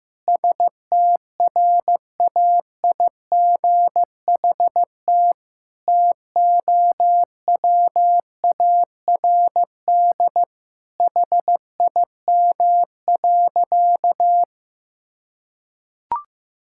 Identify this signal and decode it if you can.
Morse code, heard as a keyed tone, about 15 wpm, 700 Hz. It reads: STRAIGHT TOWARD HIM.
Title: Sentences from Top 1000 Words - Review Part 6 - 15wpm